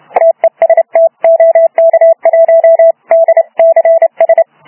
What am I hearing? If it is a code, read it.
Morse code, heard as a keyed tone, about 31 wpm, 660 Hz. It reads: TEST OK1DCS
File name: Archiv / CUC318 / CUC318 záznam OK1DCS